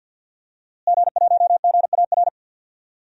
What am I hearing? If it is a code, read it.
Morse code, heard as a keyed tone, about 46 wpm, 700 Hz. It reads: G1GAR
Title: DXCC-100 Call Signs - 5 Characters - Repeated 2x - 50wpm